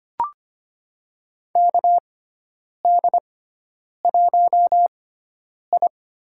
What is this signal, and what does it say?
Morse code, heard as a keyed tone, about 25 wpm, spaced as for 9 wpm, 700 Hz. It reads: KD1I
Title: USA Call Signs - 4 Characters - 6x Character Spacing - 25wpm